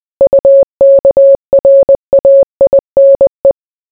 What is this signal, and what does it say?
Morse code, heard as a keyed tone, about 20 wpm, 550 Hz. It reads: UKRAINE